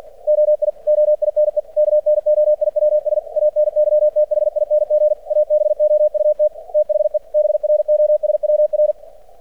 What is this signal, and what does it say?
Morse code, heard as a keyed tone, about 50 wpm, 600 Hz. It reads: OI OINE KNOIWS ANYTHING ABOUT THE BROSWR
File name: archived / CRSnet